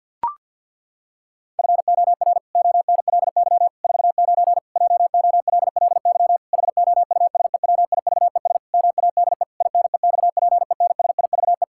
Morse code, heard as a keyed tone, about 49 wpm, 700 Hz. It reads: FOR ONLY 49 JOLLY HOUSEWIVES MADE INEXPENSIVE
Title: Pangram Challenge v2 - 50wpm